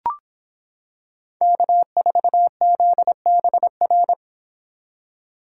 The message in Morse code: K4ZBR

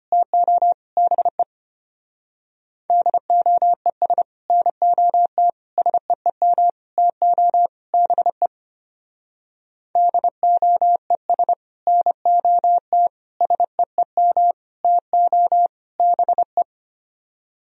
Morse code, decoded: TO BE DOES NOT SEEM TO BE DOES NOT SEEM TO BE

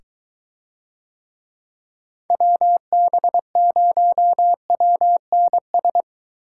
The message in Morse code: WB0WNS

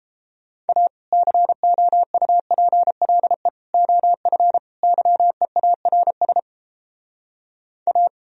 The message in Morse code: A COUPLE OF YEARS A

